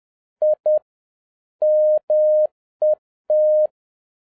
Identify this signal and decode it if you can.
Morse code, heard as a keyed tone, about 10 wpm, 610 Hz. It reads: I MET